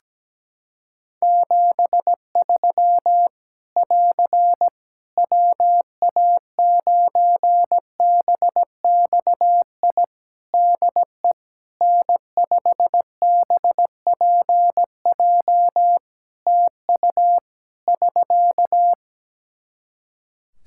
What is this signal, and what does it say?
Morse code, heard as a keyed tone, about 17 wpm, 700 Hz. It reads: 73 <AR> WA9BXI DE N5BPJ TU <SK>
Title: K1USN SST QSO Practice 17wpm